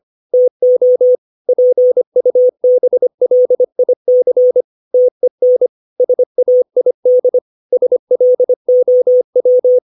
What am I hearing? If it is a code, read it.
Morse code, heard as a keyed tone, about 25 wpm, 500 Hz. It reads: TO PUBLIC TEN SAID SLOW
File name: 632 words / 632 Words 25WPM